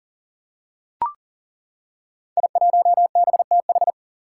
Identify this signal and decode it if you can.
Morse code, heard as a keyed tone, about 39 wpm, 700 Hz. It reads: I1BTH
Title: Five Challenging Letter-Number Combinations 40wpm